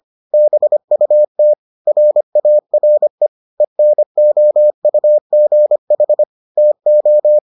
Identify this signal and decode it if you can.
Morse code, heard as a keyed tone, about 25 wpm, 600 Hz. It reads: BUT RARE ENOUGH TO